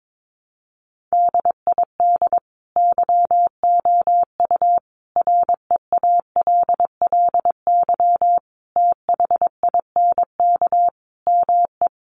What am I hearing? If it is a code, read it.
Morse code, heard as a keyed tone, about 22 wpm, 700 Hz. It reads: DID YOU REALLY THINK ME
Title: Sentences from Top 1200 Words 22wpm